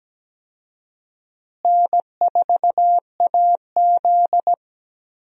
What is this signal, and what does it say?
Morse code, heard as a keyed tone, about 17 wpm, 700 Hz. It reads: N4AZ
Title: USA Call Signs - 4 Characters - Repeated 2x - 17wpm